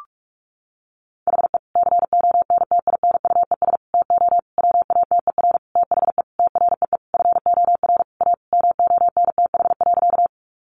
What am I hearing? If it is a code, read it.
Morse code, heard as a keyed tone, about 44 wpm, 700 Hz. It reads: HE CONTINUES TO WATER THE TREE FOR A MONTH.
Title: Easy Phrases 45wpm